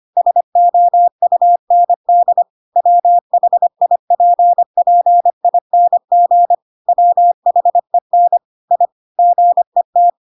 Morse code, heard as a keyed tone, about 25 wpm, 700 Hz. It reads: SOUND WHIPPING WHEN I GET
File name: Pinocchio / Pinocchio ch7 25 25 ik1hge